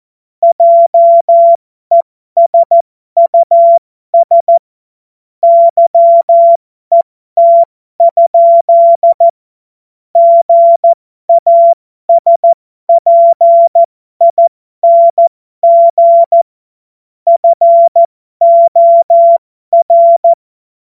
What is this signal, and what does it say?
Morse code, heard as a keyed tone, about 14 wpm, spaced as for 12 wpm, 680 Hz. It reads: JESUS YET? GASPING FOR